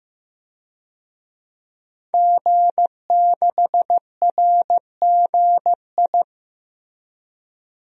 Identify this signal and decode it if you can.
Morse code, heard as a keyed tone, about 15 wpm, 700 Hz. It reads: G6RGI
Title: DXCC-100 Call Signs - 5 Characters - Repeated 2x - 15wpm